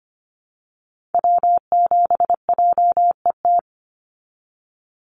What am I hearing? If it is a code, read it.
Morse code, heard as a keyed tone, about 25 wpm, 690 Hz. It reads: W7JET
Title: NA SOTA Activator Callsigns 25wpm